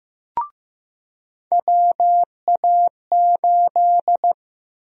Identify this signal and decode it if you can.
Morse code, heard as a keyed tone, about 15 wpm, 700 Hz. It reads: WA8
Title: USA Call Sign Prefixes - 15wpm